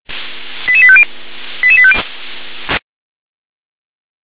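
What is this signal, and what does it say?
Five-tone selective call (ZVEI), heard as quick runs of five tones, tones of about 70 ms. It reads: 88850 88850
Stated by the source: ZVEI five-tone call received by SDR.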